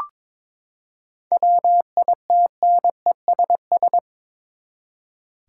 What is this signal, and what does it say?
Morse code, heard as a keyed tone, about 22 wpm, 700 Hz. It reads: WITNESS
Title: Seven Letter Words - 22wpm